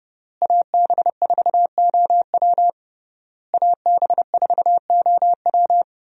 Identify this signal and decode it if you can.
Morse code, heard as a keyed tone, about 30 wpm, 700 Hz. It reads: AB4OW AB4OW
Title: USA Call Signs - 5 Characters - Repeated 3x - 30wpm